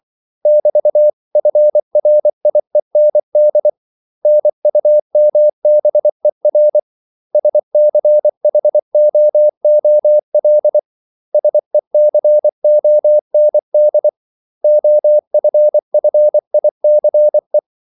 Morse code, heard as a keyed tone, about 24 wpm, 600 Hz. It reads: <BT>FRIEND NUMBER SCHOOL SECOND OFFICE
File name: CW / Practice / 500MostCommonEnglishWords24